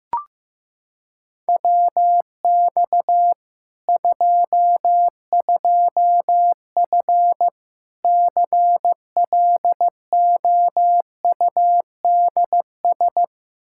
Morse code, heard as a keyed tone, about 16 wpm, 700 Hz. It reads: WX 22F CLOUDS
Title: Practice QSO - Exchange 2 - Info - 15wpm